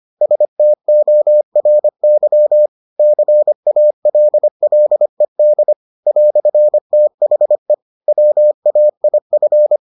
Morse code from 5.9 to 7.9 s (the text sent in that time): "THE